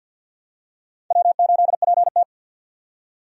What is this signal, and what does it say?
Morse code, heard as a keyed tone, about 51 wpm, 700 Hz. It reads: W8PT
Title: USA Call Signs - 4 Characters - 50wpm